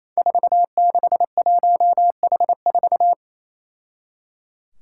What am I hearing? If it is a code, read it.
Morse code, heard as a keyed tone, about 28 wpm, 700 Hz. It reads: 461H4
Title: Five Challenging Letter-Number Combinations 28wpm